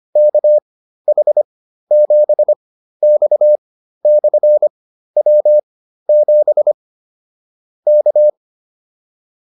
Morse code, heard as a keyed tone, about 25 wpm, spaced as for 13 wpm, 600 Hz. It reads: KH7X/W7 K
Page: intermediate-practice-files